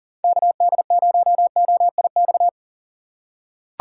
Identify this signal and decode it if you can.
Morse code, heard as a keyed tone, about 40 wpm, 700 Hz. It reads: KD0OIX
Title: K1USN SST QSO Practice - 40wpm